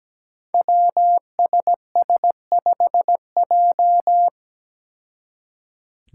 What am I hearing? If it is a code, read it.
Morse code, heard as a keyed tone, about 17 wpm, 700 Hz. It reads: WSS5J
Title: Five Challenging Letter-Number Combinations 17wpm